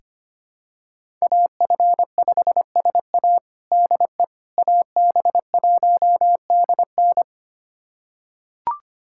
Morse code, heard as a keyed tone, about 25 wpm, 700 Hz. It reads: AF5SA DE AB1DN
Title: USA Callsign DE Callsign - 5 Characters - 25wpm